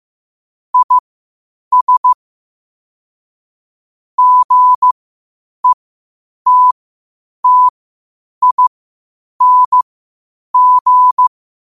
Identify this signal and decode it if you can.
Morse code, heard as a keyed tone, about 15 wpm, spaced as for 8 wpm, 1000 Hz. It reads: IS GETTING
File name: qotdfiles / 8wpm